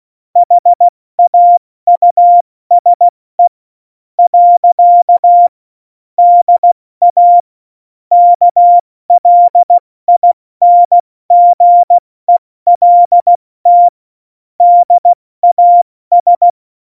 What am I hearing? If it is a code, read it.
Morse code, heard as a keyed tone, about 16 wpm, spaced as for 13 wpm, 700 Hz. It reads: HAUSE. DA KLINGELT DAS
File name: Morsekurs / Witze / Witz 05. Juni 2018, 70 F80